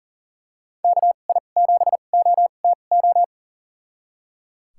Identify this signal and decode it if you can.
Morse code, heard as a keyed tone, about 37 wpm, spaced as for 28 wpm, 700 Hz. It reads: KI7OTO
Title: USA Call Signs - 6 Characters - 2x Character Spacing - 40wpm